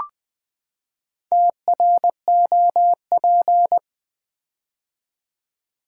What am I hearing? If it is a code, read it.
Morse code, heard as a keyed tone, about 20 wpm, 700 Hz. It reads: TROP
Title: Build A Word - 4 Letters At A Time - 8-Letter Words - 20wpm